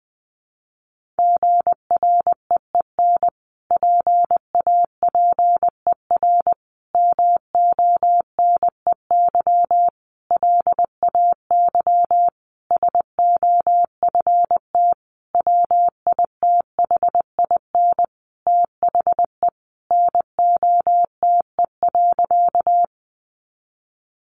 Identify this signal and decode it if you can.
Morse code, heard as a keyed tone, about 20 wpm, 700 Hz. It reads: GREEN PAPER MONEY LAY SOFT WITHIN THE NOTE.
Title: Sentences from Top 1500 Words 20wpm